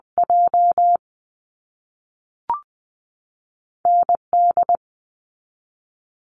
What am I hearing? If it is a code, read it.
Morse code, heard as a keyed tone, about 20 wpm, spaced as for 5 wpm, 700 Hz. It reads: JND
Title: US States 20wpm